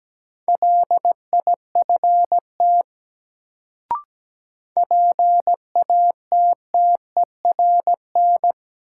Morse code, heard as a keyed tone, about 17 wpm, 700 Hz. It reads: LIFT PATTERN
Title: Sentences from Top 1000 Words - New Word Drill - 17wpm